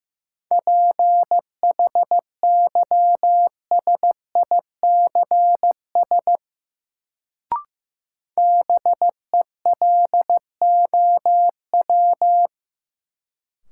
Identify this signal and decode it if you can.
Morse code, heard as a keyed tone, about 15 wpm, 700 Hz. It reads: PHYSICS BELOW
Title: Sentences from Top 1000 Words - Review Part 5 - New Word Drill - 15wpm